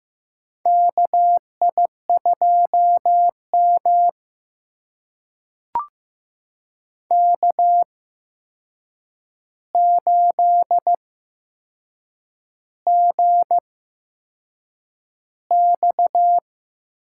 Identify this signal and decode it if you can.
Morse code, heard as a keyed tone, about 15 wpm, spaced as for 6 wpm, 700 Hz. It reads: KI2M K8GX
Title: USA Call Signs - 4 Characters - 8x Character Spacing - 15wpm